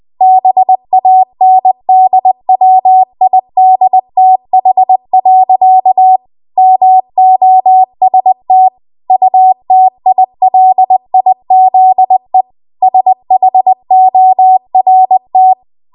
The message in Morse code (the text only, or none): BANDWIDTH. MOST UTILIZE SHORT